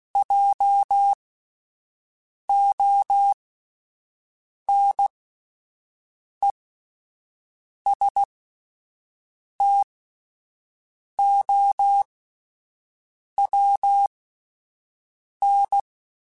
Morse code, heard as a keyed tone, about 16 wpm, spaced as for 5 wpm, 800 Hz. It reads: JONESTOWN